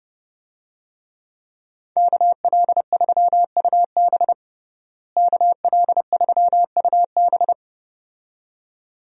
Morse code, heard as a keyed tone, about 30 wpm, 700 Hz. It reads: KL3UB KL3UB
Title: USA Call Signs - 5 Characters - Repeated 2x - 30wpm